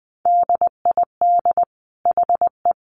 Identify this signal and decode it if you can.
Morse code, heard as a keyed tone, about 20 wpm, 700 Hz. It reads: DID HE
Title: Sentences from Top 800 Words 20wpm